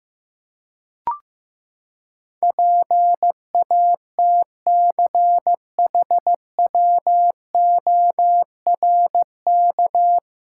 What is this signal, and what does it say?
Morse code, heard as a keyed tone, about 15 wpm, 700 Hz. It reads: PATCHWORK
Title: Pangram Challenge v2 - 15wpm